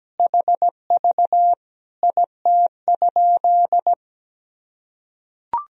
Morse code, heard as a keyed tone, about 17 wpm, 700 Hz. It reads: HV IT?
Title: Sentences from Top 100 Words and Abbreviations 17wpm